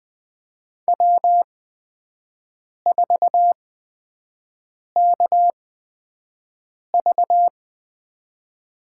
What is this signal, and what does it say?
Morse code, heard as a keyed tone, about 20 wpm, spaced as for 5 wpm, 700 Hz. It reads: W4KV
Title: USA Call Signs - 5 Characters - 8x Character Spacing - 20wpm